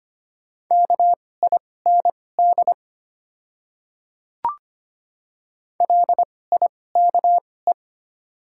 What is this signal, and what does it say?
Morse code, heard as a keyed tone, about 25 wpm, spaced as for 18 wpm, 700 Hz. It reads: KIND LIKE
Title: 2x Character Spacing - Top 300 Words - 25wpm